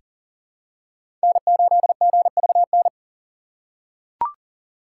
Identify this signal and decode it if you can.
Morse code, heard as a keyed tone, about 40 wpm, 700 Hz. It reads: N8GVN
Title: USA Call Signs - 5 Characters - 3x Character Spacing - 40wpm